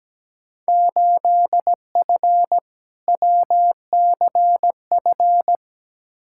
Morse code, heard as a keyed tone, about 17 wpm, 700 Hz. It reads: 8F WCF
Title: Field Day Exchanges Encore - 17wpm